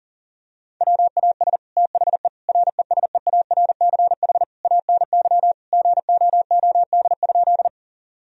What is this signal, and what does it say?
Morse code, heard as a keyed tone, about 40 wpm, 690 Hz. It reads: WAS THE RESEARCH ANY GOOD?
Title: Sentences from Top 1000 Words - Review Part 4 - 40wpm